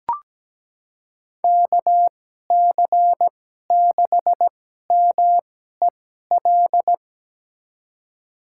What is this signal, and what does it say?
Morse code, heard as a keyed tone, about 17 wpm, spaced as for 12 wpm, 700 Hz. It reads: KC6MEL
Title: USA Call Signs - 6 Characters - 2x Character Spacing - 17wpm